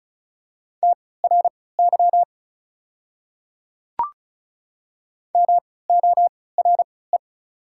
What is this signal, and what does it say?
Morse code, heard as a keyed tone, about 36 wpm, spaced as for 20 wpm, 690 Hz. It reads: TRY MORE
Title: 3x Character Spacing - Top 200 Words - 35wpm